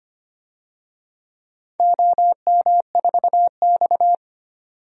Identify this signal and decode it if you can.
Morse code, heard as a keyed tone, about 25 wpm, 700 Hz. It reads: OM4X
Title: DXCC-100 Call Signs - 4 Characters - 25wpm